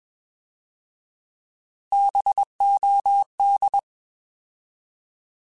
Morse code, heard as a keyed tone, about 21 wpm, 800 Hz. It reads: BOD
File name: Bs @20